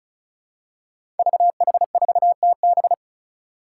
Morse code, heard as a keyed tone, about 36 wpm, 700 Hz. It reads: VH4TB
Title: Five Challenging Letter-Number Combinations 35wpm